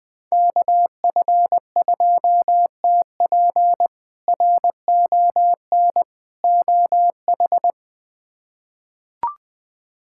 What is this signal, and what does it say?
Morse code, heard as a keyed tone, about 21 wpm, 700 Hz. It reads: KF2TP RON OH
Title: K1USN SST QSO Practice - 20wpm